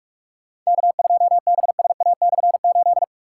Morse code, heard as a keyed tone, about 46 wpm, 700 Hz. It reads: K2BSA/8
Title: K1USN SST QSO Practice - 45wpm